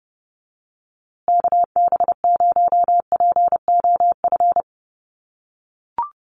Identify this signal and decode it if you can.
Morse code, heard as a keyed tone, about 30 wpm, 700 Hz. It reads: KB0POF
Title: USA Call Signs Repeated 2x 30wpm